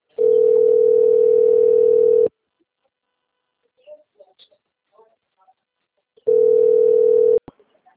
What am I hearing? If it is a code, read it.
Ringback tone (North America), 440 + 480 Hz.
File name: rg-1600-unknown- - - .11436